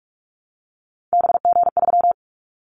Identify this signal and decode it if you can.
Morse code, heard as a keyed tone, about 44 wpm, 700 Hz. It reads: BG3
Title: Callsign Prefixes for Top 100 DXCC - 45wpm